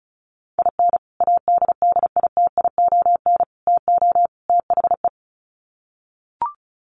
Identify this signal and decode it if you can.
Morse code, heard as a keyed tone, about 34 wpm, 700 Hz. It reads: IN ADDITION TO THE
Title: Sets of 4 Words 35wpm